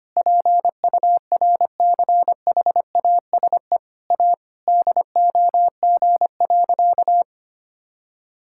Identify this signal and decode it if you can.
Morse code, heard as a keyed tone, about 25 wpm, 700 Hz. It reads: PURCHASE A DOG.
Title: Sentences from Top 1000 Words - Review Part 5 - 25wpm